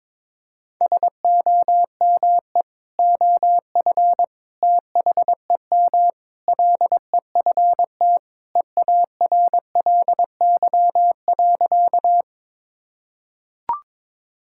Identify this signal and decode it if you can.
Morse code, heard as a keyed tone, about 22 wpm, 700 Hz. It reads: SOME OF THEM LEFT EARLY.
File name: practice / 22wpm / Sentences with 3-Word Phrases - 22wpm